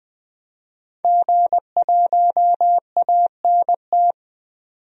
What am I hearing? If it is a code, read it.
Morse code, heard as a keyed tone, about 20 wpm, 700 Hz. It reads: G1ANT
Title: DXCC-100 Call Signs - 5 Characters - 20wpm